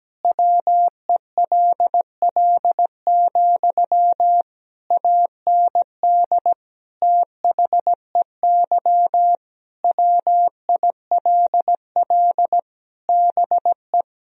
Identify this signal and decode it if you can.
Morse code, heard as a keyed tone, about 17 wpm, 700 Hz. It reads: WELL, AND THEY WILL BE
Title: Sentences from Top 1000 Words - Review Part 2 - 17wpm